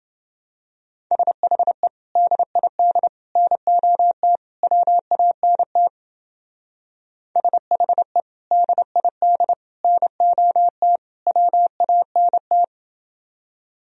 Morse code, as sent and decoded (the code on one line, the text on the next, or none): ... .... / -.. .. -.. / -. --- - / .-- .- -. - / ... .... / -.. .. -.. / -. --- - / .-- .- -. -
SHE DID NOT WANT SHE DID NOT WANT